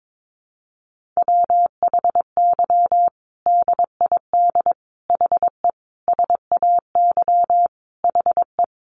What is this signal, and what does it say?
Morse code, heard as a keyed tone, about 22 wpm, 700 Hz. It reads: WHY DID HE SAY HE
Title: Sentences from Top 1500 Words 22wpm